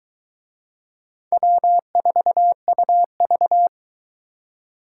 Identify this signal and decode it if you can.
Morse code, heard as a keyed tone, about 23 wpm, 700 Hz. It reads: W4UV